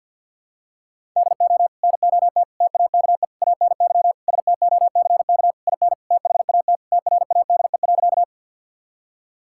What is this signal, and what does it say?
Morse code, heard as a keyed tone, about 50 wpm, 700 Hz. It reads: DO NOT TAKE ANY STOCK IN THAT TRADE.
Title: Sentences from Top 1000 Words - Review Part 5 - 50wpm